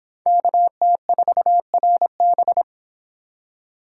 Morse code, heard as a keyed tone, about 29 wpm, 700 Hz. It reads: KT4RB